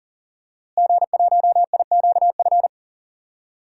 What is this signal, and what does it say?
Morse code, heard as a keyed tone, about 40 wpm, 700 Hz. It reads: G1IQF
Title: DXCC-100 Call Signs - 5 Characters - Repeated 2x - 40wpm